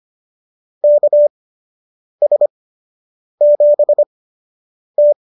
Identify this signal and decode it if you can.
Morse code, heard as a keyed tone, about 25 wpm, spaced as for 8 wpm, 590 Hz. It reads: KS7T